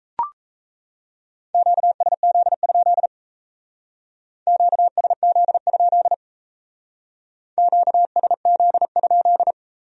Morse code, heard as a keyed tone, about 37 wpm, 700 Hz. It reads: QSZ? QSZ? QSZ?